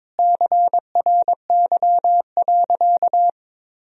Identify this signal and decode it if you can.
Morse code, heard as a keyed tone, about 22 wpm, 700 Hz. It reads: CRY.
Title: Sentences from Top 1000 Words - Review Part 2 - 22wpm